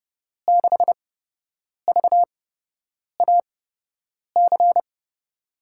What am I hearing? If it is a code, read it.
Morse code, heard as a keyed tone, about 30 wpm, spaced as for 8 wpm, 700 Hz. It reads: 6VAC
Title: USA Call Signs - 8x Character Spacing - 30wpm